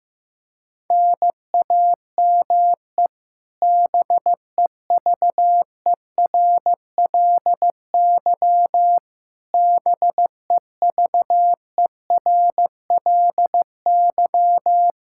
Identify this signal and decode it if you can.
Morse code, heard as a keyed tone, about 15 wpm, 700 Hz. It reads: NAME BEVERLY BEVERLY
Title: Practice QSO - Exchange 1 - Beginning - 15wpm